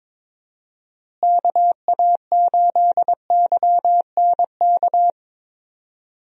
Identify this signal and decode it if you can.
Morse code, heard as a keyed tone, about 22 wpm, 700 Hz. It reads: KA8YNK
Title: USA Call Signs - 6 Characters - Repeated 3x - 22wpm